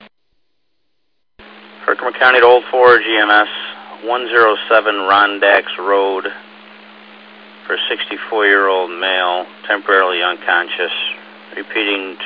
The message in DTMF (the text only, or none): none